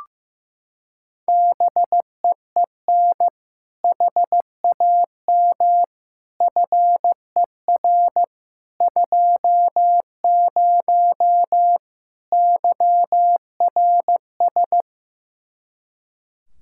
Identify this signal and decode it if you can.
Morse code, heard as a keyed tone, about 15 wpm, 700 Hz. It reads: BEEN HAM FER 20 YRS
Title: Practice QSO - Exchange 3 - More Info - 15wpm